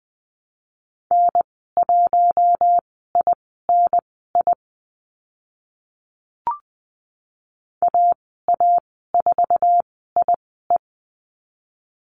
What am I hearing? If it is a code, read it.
Morse code, heard as a keyed tone, about 20 wpm, spaced as for 14 wpm, 700 Hz. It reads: N1INI AA4IE
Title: 048 Lesson 9g - Number 4 - Call Signs 20wpm